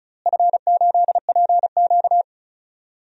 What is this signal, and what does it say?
Morse code, heard as a keyed tone, about 34 wpm, 700 Hz. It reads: F8PQ
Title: DXCC-100 Call Signs - 4 Characters - Repeated 3x - 35wpm